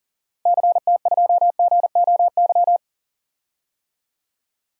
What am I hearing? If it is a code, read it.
Morse code, heard as a keyed tone, about 40 wpm, 700 Hz. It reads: CT2GOY
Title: DXCC-100 Call Signs - 6 Characters - Repeated 2x - 40wpm